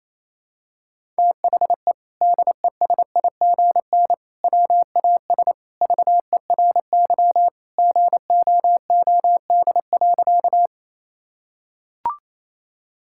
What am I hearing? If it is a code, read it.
Morse code, heard as a keyed tone, about 27 wpm, 700 Hz. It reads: THE DESIGN WAS VERY GOOD.
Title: Sentences from Top 1000 Words - Review Part 5 - 28wpm